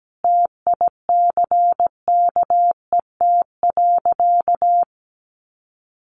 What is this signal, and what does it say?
Morse code, heard as a keyed tone, about 17 wpm, 700 Hz. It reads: TICKET.
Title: Sentences from Top 1400 Words 17wpm